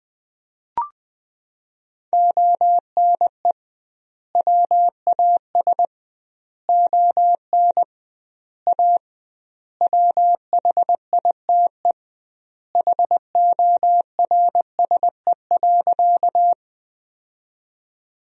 Morse code, heard as a keyed tone, about 20 wpm, 700 Hz. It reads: ONE WAS ON A WHITE HORSE.
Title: 2x Word-Spacing - Sentences from Top 300 Words - 20wpm